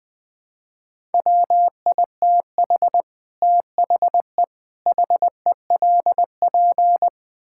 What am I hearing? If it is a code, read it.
Morse code, heard as a keyed tone, about 20 wpm, 700 Hz. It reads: WITH THE HELP
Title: Build A Phrase - 5-Word Phrases - 20wpm